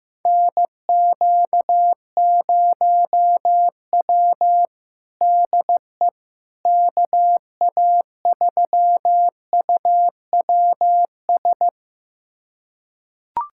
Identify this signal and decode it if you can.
Morse code, heard as a keyed tone, about 15 wpm, 700 Hz. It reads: NQ0W DE KA3UWS
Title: USA Callsign DE Callsign 15wpm